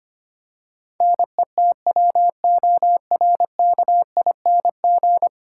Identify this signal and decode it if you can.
Morse code, heard as a keyed tone, about 25 wpm, 700 Hz. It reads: NETWORKING
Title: Practice QSO - Exchange 3 - More Info - 25wpm